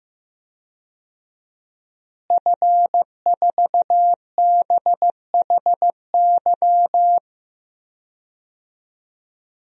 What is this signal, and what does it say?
Morse code, heard as a keyed tone, about 15 wpm, 700 Hz. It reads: F4BHY
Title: DXCC-100 Call Signs - 5 Characters - Repeated 3x - 15wpm